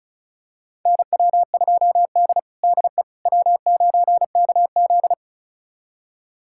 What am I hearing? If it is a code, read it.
Morse code, heard as a keyed tone, about 36 wpm, 700 Hz. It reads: NW2D DE W9KZ
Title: USA Callsign DE Callsign - 4 Characters - 35wpm